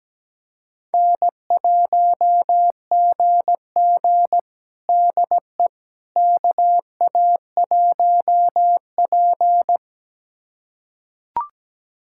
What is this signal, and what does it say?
Morse code, heard as a keyed tone, about 17 wpm, 700 Hz. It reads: N1GG DE KA1P
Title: USA Callsign DE Callsign - 4 Characters - 17wpm